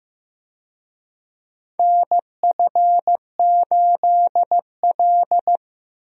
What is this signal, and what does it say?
Morse code, heard as a keyed tone, about 15 wpm, 700 Hz. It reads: NF8L
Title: USA Call Signs - 4 Characters - 15wpm